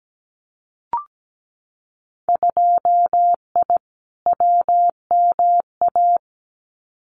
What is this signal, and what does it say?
Morse code, heard as a keyed tone, about 17 wpm, 700 Hz. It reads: 2I WMA
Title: Winter Field Day Exchanges Encore 17wpm